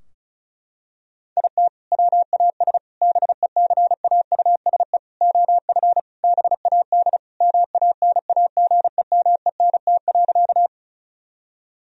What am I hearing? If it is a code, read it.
Morse code, heard as a keyed tone, about 36 wpm, 700 Hz. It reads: IT WAS BECAUSE OF BAD MANAGEMENT.
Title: Sentences from Top 1000 Words - Review Part 4 - 35wpm